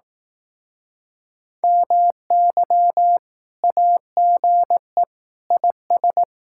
Morse code, heard as a keyed tone, about 18 wpm, 700 Hz. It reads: MY AGE IS
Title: Speed-Racing Common QSO Phrases Encore - 15wpm